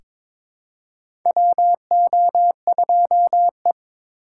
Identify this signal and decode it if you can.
Morse code, heard as a keyed tone, about 22 wpm, 700 Hz. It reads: WO2E
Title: K1USN SST QSO Practice - 22wpm